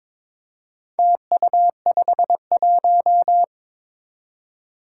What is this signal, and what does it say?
Morse code, heard as a keyed tone, about 22 wpm, 700 Hz. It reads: TU51
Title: Four Challenging Letter-Number Combinations 22wpm